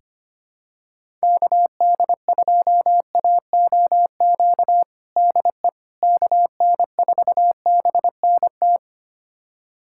Morse code, heard as a keyed tone, about 25 wpm, 700 Hz. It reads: KD2AOQ DE KN4BNT
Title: USA Callsign DE Callsign - 6 Characters - 25wpm